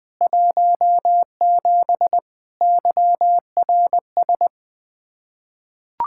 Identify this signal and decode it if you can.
Morse code, heard as a keyed tone, about 21 wpm, 700 Hz. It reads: 17 YRS
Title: Practice QSO - Exchange 3 - More Info - 20wpm